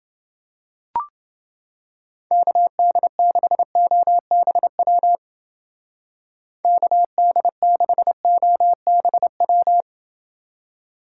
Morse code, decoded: KD6OBW KD6OBW

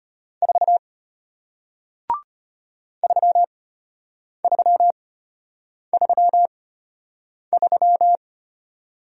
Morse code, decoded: 4 3333

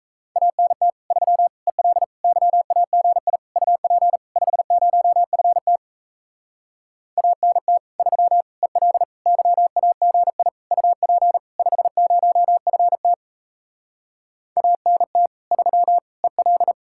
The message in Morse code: ANT 3 EL YAGI UP 50FT ANT 3 EL YAGI UP 50FT ANT 3 EL

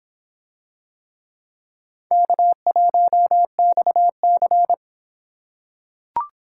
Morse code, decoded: K1XC

